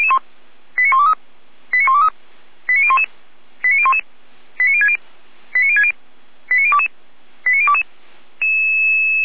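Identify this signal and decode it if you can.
Five-tone selective call (ZVEI), heard as quick runs of five tones, tones of about 70 ms. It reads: 89123 89123 89011 89011 89060 89060 89022 89022
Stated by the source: ZVEI five-tone selective call received by SDR.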